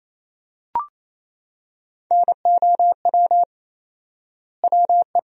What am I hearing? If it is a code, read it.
Morse code, heard as a keyed tone, about 28 wpm, 700 Hz. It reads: NOW WE
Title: 4x Word-Spacing - Sentences from Top 500 Words - 28wpm